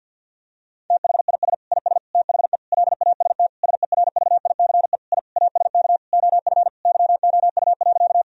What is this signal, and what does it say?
Morse code, heard as a keyed tone, about 50 wpm, 700 Hz. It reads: THIS IS THE LAST SERVICE I ASK OF YOU.
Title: Sentences from Top 1000 Words - Review Part 4 - 50wpm